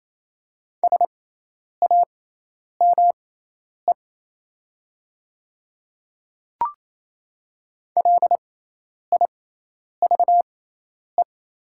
Morse code, heard as a keyed tone, about 28 wpm, spaced as for 10 wpm, 700 Hz. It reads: SAME LIVE
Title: 6x Character Spacing - Top 500 Words - 28wpm